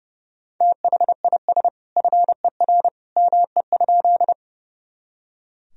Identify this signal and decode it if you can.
Morse code, heard as a keyed tone, about 32 wpm, 700 Hz. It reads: THIS FER ME?
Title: Sentences from Top 200 Words and Abbreviations 30wpm